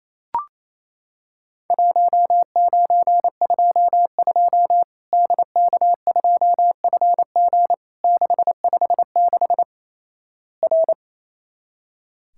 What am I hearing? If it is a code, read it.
Morse code, heard as a keyed tone, about 28 wpm, 700 Hz. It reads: 1922 DK2FG 656 R
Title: Worked All Europe DX Contest - QTC Exchange - 28wpm